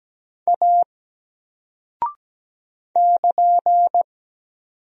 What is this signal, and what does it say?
Morse code, heard as a keyed tone, about 17 wpm, 700 Hz. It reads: A <KN>
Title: Single Character (Letter:Number:Punctuation:Prosign) 17wpm